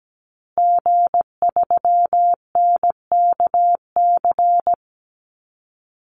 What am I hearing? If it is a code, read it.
Morse code, heard as a keyed tone, about 17 wpm, 700 Hz. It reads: G3NKC
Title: Call Signs Repeated 3x - 17wpm